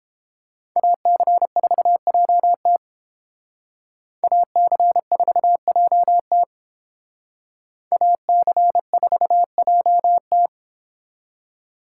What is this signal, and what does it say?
Morse code, heard as a keyed tone, about 30 wpm, 700 Hz. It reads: AC4JT AC4JT AC4JT